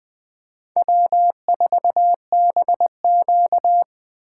Morse code, heard as a keyed tone, about 19 wpm, 700 Hz. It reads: W4BQ